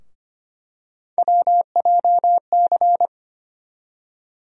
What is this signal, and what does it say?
Morse code, heard as a keyed tone, about 25 wpm, 700 Hz. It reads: WJC
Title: USA Call Sign Postfixes - 3 Letters 25wpm